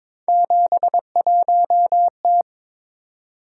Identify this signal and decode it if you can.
Morse code, heard as a keyed tone, about 23 wpm, 700 Hz. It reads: 71T
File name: practice / 22wpm / Three Letter-Number Combinations - 22wpm